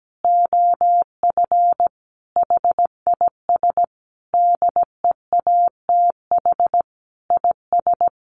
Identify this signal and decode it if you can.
Morse code, heard as a keyed tone, about 17 wpm, 700 Hz. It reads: OF HIS DEATH IS
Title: Sentences from Top 1100 Words 17wpm